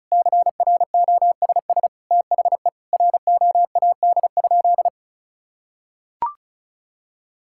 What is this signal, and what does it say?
Morse code, heard as a keyed tone, about 36 wpm, 700 Hz. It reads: CROSS THE ROAD?
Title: Sentences from Top 1000 Words - Review Part 5 - 35wpm